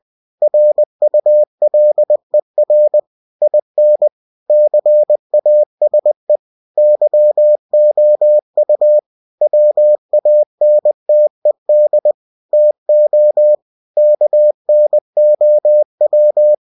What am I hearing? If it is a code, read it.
Morse code, heard as a keyed tone, about 20 wpm, 600 Hz. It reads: RULER IN CASE YOU WANTED TO KNOW
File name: cw-books / winnie-the-pooh / winnie-20wpm-page-0109